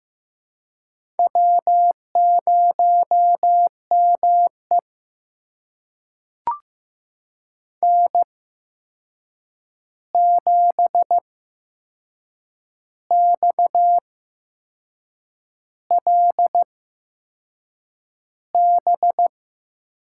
Morse code, heard as a keyed tone, about 15 wpm, spaced as for 6 wpm, 700 Hz. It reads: W0ME N7XLB
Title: USA Call Signs - 8x Character Spacing - 15wpm